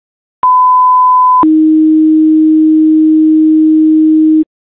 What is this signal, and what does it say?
Two-tone page, heard as a one-second tone, then a three-second tone, about 1010 then 310 Hz. It is heard from 0.4 to 4.4 s.